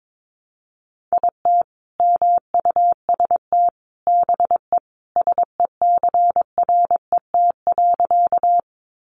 Morse code, heard as a keyed tone, about 22 wpm, 700 Hz. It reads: IT MUST BE SECRET.
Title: Sentences from Top 1200 Words 22wpm